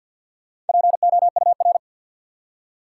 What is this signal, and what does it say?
Morse code, heard as a keyed tone, about 47 wpm, 700 Hz. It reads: POUR
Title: Four Letter Words - 50wpm